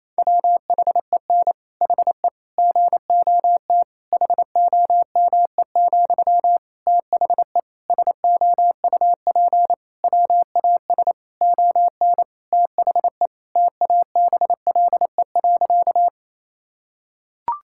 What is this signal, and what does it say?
Morse code, heard as a keyed tone, about 27 wpm, 700 Hz. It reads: WHEN HE GOT HOME, THE SOUP WAS ON THE TABLE.
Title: Sentences from Top 1000 Words - Review Part 6 - 28wpm